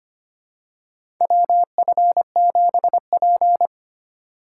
Morse code, heard as a keyed tone, about 25 wpm, 700 Hz. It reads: WF7P
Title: USA Call Signs - 4x Character Spacing - 25wpm